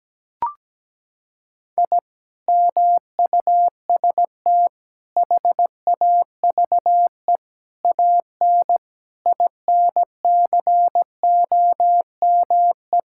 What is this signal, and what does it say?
Morse code, heard as a keyed tone, about 17 wpm, 700 Hz. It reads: I MUST HAVE AN INCOME
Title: Sentences from Top 1000 Words - Review Part 5 - 17wpm